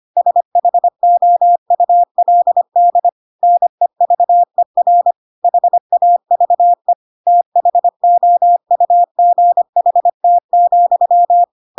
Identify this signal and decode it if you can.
Morse code, heard as a keyed tone, about 25 wpm, 700 Hz. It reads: SHOULD NEVER HAVE THOUGHT,
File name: Pinocchio / Pinocchio ch7 25 25 ik1hge